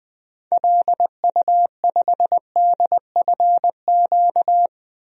Morse code, heard as a keyed tone, about 20 wpm, 700 Hz. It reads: LU5DFQ
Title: DXCC-100 Call Signs - 6 Characters - 20wpm